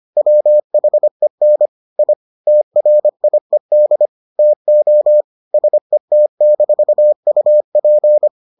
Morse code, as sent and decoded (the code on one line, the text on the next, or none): .-- .... -. / .. / - .-. .. -.. / - --- / ... - -....- ..- .--.
WHEN I TRIED TO SET-UP